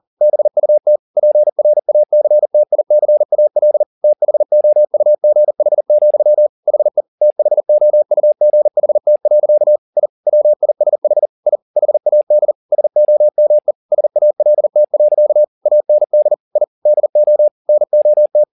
Morse code, heard as a keyed tone, about 40 wpm, 600 Hz. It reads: BUT PRACTICAL THOUGH, HE THOUGHT. I WISH I HAD SOME SALT. AND I DO NOT